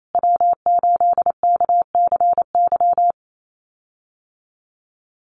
Morse code, heard as a keyed tone, about 28 wpm, 700 Hz. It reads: W8KCY
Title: USA Call Signs Repeated 3x 28wpm